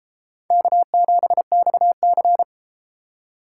Morse code, heard as a keyed tone, about 33 wpm, 700 Hz. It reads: K7XC